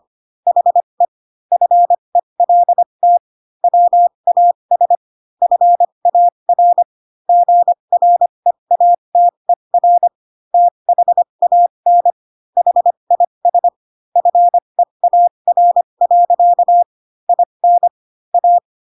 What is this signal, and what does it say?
Morse code, heard as a keyed tone, about 25 wpm, spaced as for 22 wpm, 700 Hz. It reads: HE FELT WAS FAR GREATER THAN HIS FEAR. IN A
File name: Pinocchio / Pinocchio ch6 18 25 ik1hge